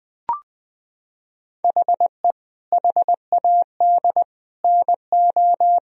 Morse code, heard as a keyed tone, about 20 wpm, 700 Hz. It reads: HE HAD NO
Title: Sentences from Top 1000 Words - Review Part 1 - 20wpm